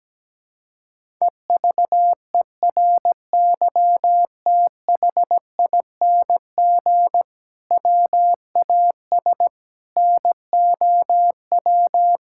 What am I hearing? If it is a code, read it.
Morse code, heard as a keyed tone, about 17 wpm, 700 Hz. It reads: EVERYTHING WAS NOW
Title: Sentences from Top 1000 Words - Review Part 5 - 17wpm